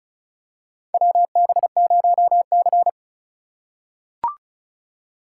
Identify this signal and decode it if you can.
Morse code, heard as a keyed tone, about 36 wpm, 700 Hz. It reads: WB0C
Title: USA Call Signs - 4x Character Spacing - 35wpm